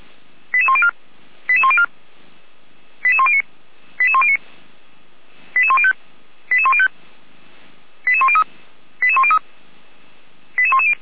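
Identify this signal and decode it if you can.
Five-tone selective call (ZVEI), heard as quick runs of five tones, tones of about 70 ms. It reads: 88104 88104 88109 88109 88105 88105 88103 88103 88110
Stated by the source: Five-tone ZVEI selective call received on an SDR by feuerwehr-orlamuende (fire brigade Orlamünde).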